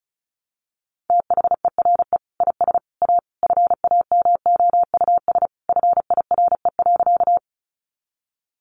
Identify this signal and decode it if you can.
Morse code, heard as a keyed tone, about 34 wpm, 700 Hz. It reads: THERE IS A FAMOUS FIRE.
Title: Sentences from Top 1500 Words 35wpm